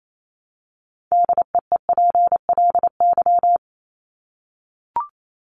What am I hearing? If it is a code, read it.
Morse code, heard as a keyed tone, about 28 wpm, 700 Hz. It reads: DEEPLY
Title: Top 1751-2000 Words 28wpm